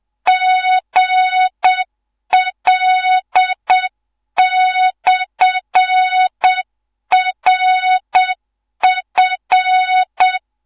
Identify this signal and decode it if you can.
Morse code, heard as a keyed tone, about 7 wpm, 3700 Hz. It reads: GL/RF